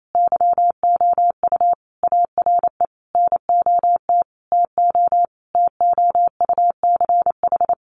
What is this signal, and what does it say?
Morse code, heard as a keyed tone, about 28 wpm, 700 Hz. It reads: YOU ARE NOT TO TOUCH
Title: Sentences from Top 1200 Words 28wpm